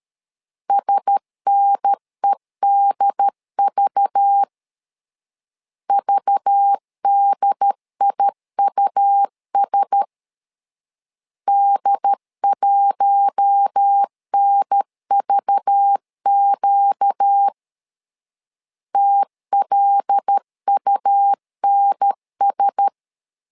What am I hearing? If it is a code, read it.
Morse code, heard as a keyed tone, about 13 wpm, 800 Hz. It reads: SNEDV VDIUS D1NVQ TLUNS
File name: LEZIONI / 5 LBVC / 5 LBV 12wpm 1855